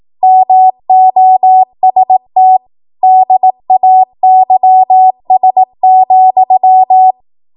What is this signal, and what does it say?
Morse code, heard as a keyed tone, about 18 wpm, 750 Hz. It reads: MOST DAYS,
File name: Morse / 18WPM